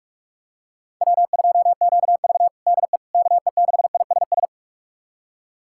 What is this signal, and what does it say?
Morse code, heard as a keyed tone, about 46 wpm, 700 Hz. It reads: W2QV DE KE6ISS
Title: USA Callsign DE Callsign 45wpm